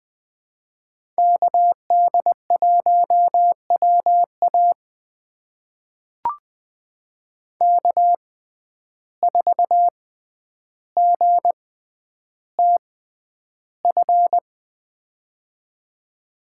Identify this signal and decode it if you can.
Morse code, heard as a keyed tone, about 20 wpm, spaced as for 10 wpm, 700 Hz. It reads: KD1WA K4GTF
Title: USA Call Signs - 5 Characters - 6x Character Spacing - 20wpm